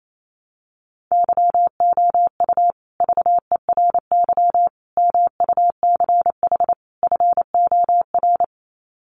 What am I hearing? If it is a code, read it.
Morse code, heard as a keyed tone, about 28 wpm, 700 Hz. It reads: YOU VERY MUCH FOR
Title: Sets of 4 Words 28wpm